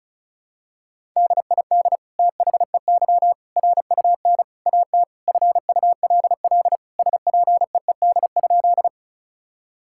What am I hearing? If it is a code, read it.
Morse code, heard as a keyed tone, about 36 wpm, 700 Hz. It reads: DID THEY RUN AT FULL SPEED?
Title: Sentences from Top 1000 Words - Review Part 5 - 35wpm